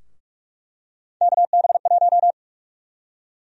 Morse code, heard as a keyed tone, about 42 wpm, 700 Hz. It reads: KB1
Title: USA Call Sign Prefixes - Repeated 2x - 45wpm